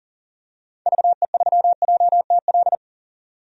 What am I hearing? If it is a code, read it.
Morse code, heard as a keyed tone, about 40 wpm, 700 Hz. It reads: VE3JTL
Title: DXCC-100 Call Signs - 6 Characters - 40wpm